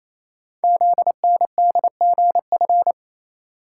Morse code, heard as a keyed tone, about 29 wpm, 700 Hz. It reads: ZNDGF
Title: Code Groups - 5 Letters - 28wpm